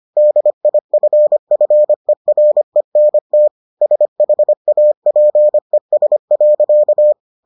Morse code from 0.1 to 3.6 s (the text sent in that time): DIFFERENT